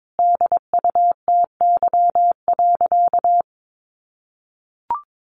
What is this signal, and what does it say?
Morse code, heard as a keyed tone, about 22 wpm, 700 Hz. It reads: DUTY.
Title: Sentences from Top 1500 Words 22wpm